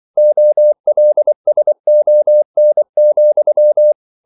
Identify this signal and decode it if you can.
Morse code, heard as a keyed tone, about 24 wpm, 600 Hz. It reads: OLSON,
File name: CW / Practice / EnglishQuotations0424